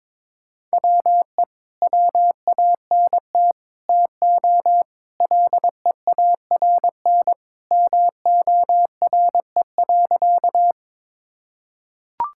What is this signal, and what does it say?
Morse code, heard as a keyed tone, about 22 wpm, 700 Hz. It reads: WE WANT TO LEARN MORE.
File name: practice / 22wpm / Sentences with 3-Word Phrases - 22wpm